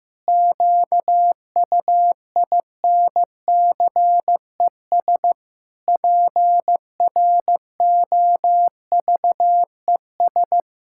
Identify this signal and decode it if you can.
Morse code, heard as a keyed tone, about 15 wpm, 700 Hz. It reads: QUINCES PROVES
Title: Pangram Challenge v2 - 15wpm